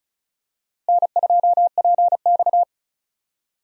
Morse code, heard as a keyed tone, about 36 wpm, 700 Hz. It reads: N2PX
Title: USA Call Signs - 4 Characters - Repeated 3x - 35wpm